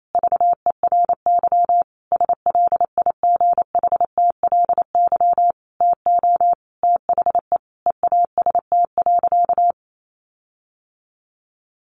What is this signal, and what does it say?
Morse code, heard as a keyed tone, about 28 wpm, 700 Hz. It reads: VERY SLIGHTLY TO THE EAST.
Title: Sentences from Top 1300 Words 28wpm